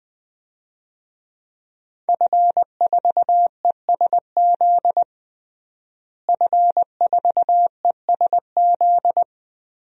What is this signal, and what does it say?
Morse code, heard as a keyed tone, about 20 wpm, 700 Hz. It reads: F4ESZ F4ESZ
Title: DXCC-100 Call Signs - 5 Characters - Repeated 2x - 20wpm